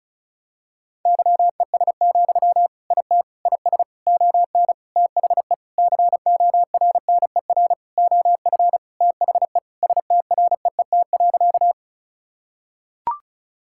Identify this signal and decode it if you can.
Morse code, heard as a keyed tone, about 36 wpm, 700 Hz. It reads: YES, IT IS ON THE CORNER OF THE STREET.
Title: Sentences from Top 1600 Words 35wpm